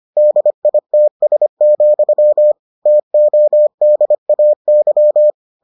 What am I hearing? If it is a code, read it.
Morse code, heard as a keyed tone, about 25 wpm, 600 Hz. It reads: DITS, TODAY